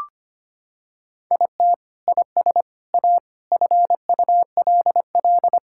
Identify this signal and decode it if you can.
Morse code, heard as a keyed tone, about 25 wpm, 700 Hz. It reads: IT IS A FULL
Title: Sentences from Top 1000 Words - Review Part 5 - 25wpm